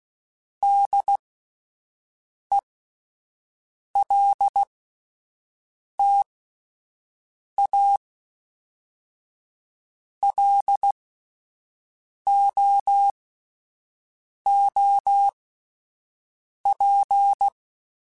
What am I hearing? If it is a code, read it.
Morse code, heard as a keyed tone, about 16 wpm, spaced as for 5 wpm, 800 Hz. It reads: DELTA LOOP
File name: fromSMs / FromSMs mp3s / Antennas@5